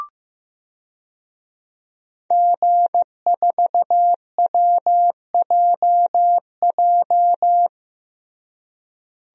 Morse code, heard as a keyed tone, about 14 wpm, 700 Hz. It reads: G4WJJ
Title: DXCC-100 Call Signs - 5 Characters - Repeated 3x - 15wpm